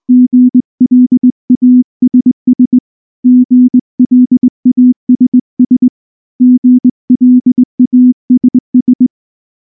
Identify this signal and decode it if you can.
Morse code, heard as a keyed tone, about 19 wpm, 250 Hz. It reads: GLASS GLASS GLASS
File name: morse / morse-training-20WPM-5